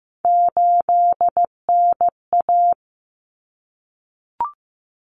Morse code, heard as a keyed tone, about 15 wpm, 700 Hz. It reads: 8NA
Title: Common Cut Numbers - 3-Digit 15wpm